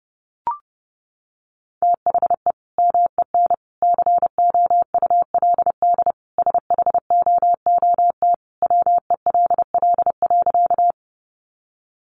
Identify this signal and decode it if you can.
Morse code, heard as a keyed tone, about 30 wpm, 700 Hz. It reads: THE MEN COULD SHOOT WELL.
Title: Sentences from Top 1200 Words 30wpm